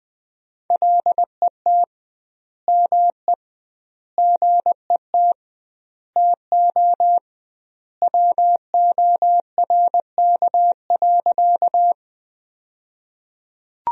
Morse code, heard as a keyed tone, about 20 wpm, 700 Hz. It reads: LET ME GET TO WORK.
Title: 2x Word-Spacing - Sentences from Top 300 Words - 20wpm